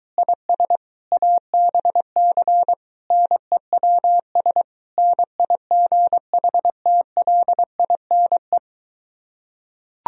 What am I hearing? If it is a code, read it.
Morse code, heard as a keyed tone, about 23 wpm, 700 Hz. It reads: IS ABC NEWS NIGHTLINE